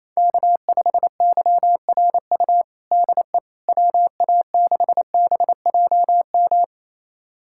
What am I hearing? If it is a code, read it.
Morse code, heard as a keyed tone, about 28 wpm, 700 Hz. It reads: K5YRU DE WA6BJM
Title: USA Callsign DE Callsign 28wpm